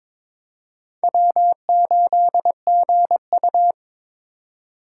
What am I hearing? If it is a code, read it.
Morse code, heard as a keyed tone, about 22 wpm, 700 Hz. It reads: W8GU
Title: K1USN SST QSO Practice - 22wpm